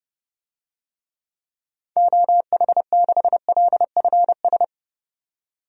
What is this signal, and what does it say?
Morse code, heard as a keyed tone, about 31 wpm, 700 Hz. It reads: OH6LFS
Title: DXCC-100 Call Signs - 6 Characters - 30wpm